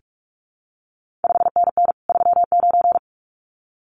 Morse code, heard as a keyed tone, about 46 wpm, 700 Hz. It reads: 5NN 39
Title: CQ WW DX Contest Exchange - 45wpm